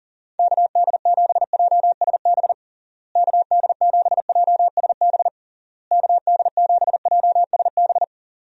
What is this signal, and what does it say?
Morse code, heard as a keyed tone, about 37 wpm, 700 Hz. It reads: KD7JSB KD7JSB KD7JSB
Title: USA Call Signs - 6 Characters - Repeated 3x - 40wpm